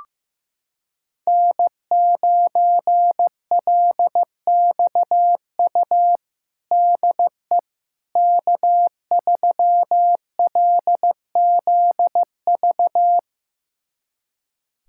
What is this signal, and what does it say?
Morse code, heard as a keyed tone, about 15 wpm, 700 Hz. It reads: N9LXU DE K3LZV
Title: USA Callsign DE Callsign - 5 Characters - 15wpm